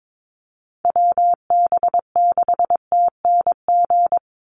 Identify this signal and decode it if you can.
Morse code, heard as a keyed tone, about 22 wpm, 700 Hz. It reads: WB6TNG
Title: USA Call Signs 22wpm